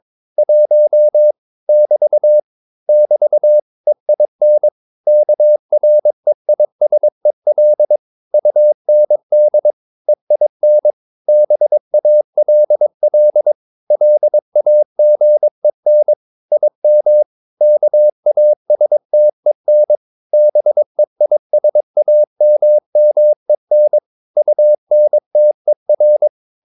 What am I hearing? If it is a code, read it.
Morse code, heard as a keyed tone, about 22 wpm, 600 Hz. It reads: 1 <BT> <BT> EIN KREISEL UND EIN BALL LAGEN IM KASTEN BEISAMMEN UNTER